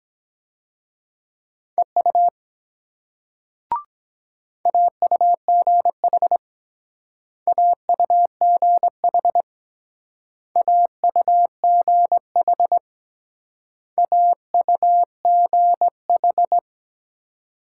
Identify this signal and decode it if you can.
Morse code, heard as a keyed tone, about 22 wpm, 700 Hz. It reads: EU AUGH AUGH AUGH AUGH